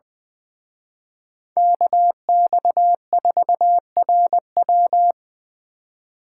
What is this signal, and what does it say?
Morse code, heard as a keyed tone, about 24 wpm, 700 Hz. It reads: KX4RW
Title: Speed Racing - USA 2x2 Call Signs 20wpm